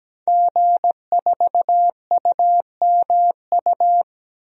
Morse code, heard as a keyed tone, about 17 wpm, 700 Hz. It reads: G4UMU
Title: DXCC-100 Call Signs - 5 Characters - 17wpm